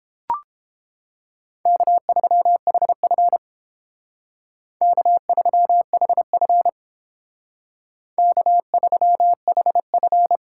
Morse code, heard as a keyed tone, about 31 wpm, 700 Hz. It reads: K3HF K3HF K3HF